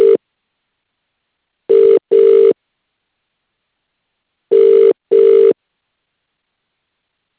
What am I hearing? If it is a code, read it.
Ringback tone (UK), 400 + 450 Hz.